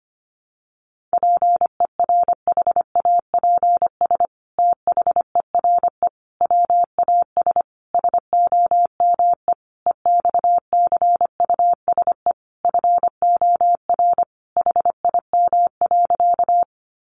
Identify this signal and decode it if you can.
Morse code, heard as a keyed tone, about 25 wpm, 700 Hz. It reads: PERHAPS THERE WAS SOME EXCUSE FOR HIM.
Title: Sentences from Top 700 Words 25wpm